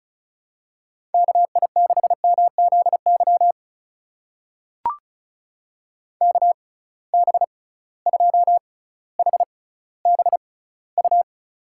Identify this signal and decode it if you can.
Morse code, heard as a keyed tone, about 36 wpm, spaced as for 18 wpm, 700 Hz. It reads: KI6MZY KB2HBU
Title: USA Call Signs - 6 Characters - 6x Character Spacing - 35wpm